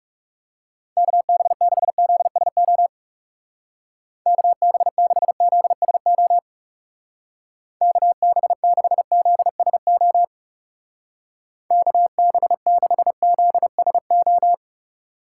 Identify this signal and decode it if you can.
Morse code, heard as a keyed tone, about 36 wpm, 700 Hz. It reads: KB6ZSO KB6ZSO KB6ZSO KB6ZSO
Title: Speed Racing - USA 2x3 Call Signs 30wpm